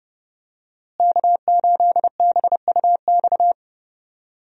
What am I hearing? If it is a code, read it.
Morse code, heard as a keyed tone, about 32 wpm, 700 Hz. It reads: K8BUX
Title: USA Call Signs - 5 Characters - Repeated 3x - 30wpm